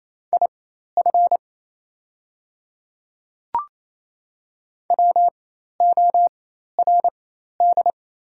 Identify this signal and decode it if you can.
Morse code, heard as a keyed tone, about 28 wpm, spaced as for 13 wpm, 690 Hz. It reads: IF WORD
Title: 4x Character Spacing - Top 200 Words - 28wpm